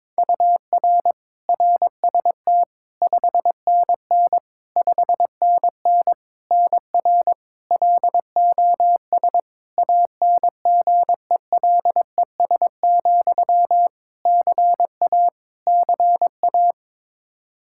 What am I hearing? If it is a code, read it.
Morse code, heard as a keyed tone, about 22 wpm, 700 Hz. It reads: UR RST 5NN 5NN NR LOS ANGELES, CA CA
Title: K1USN SST QSO Practice 22wpm